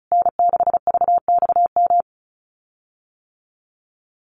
Morse code, heard as a keyed tone, about 34 wpm, 700 Hz. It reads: N6VXM
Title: USA Call Signs Repeated 3x 35wpm